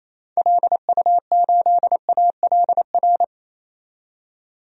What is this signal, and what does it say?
Morse code, heard as a keyed tone, about 28 wpm, 700 Hz. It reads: LU8ALR
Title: DXCC-100 Call Signs - 6 Characters - 28wpm